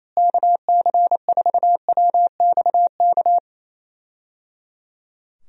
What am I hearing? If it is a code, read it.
Morse code, heard as a keyed tone, about 28 wpm, 700 Hz. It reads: KC4WXK
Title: USA Call Signs - 6 Characters - Repeated 3x - 28wpm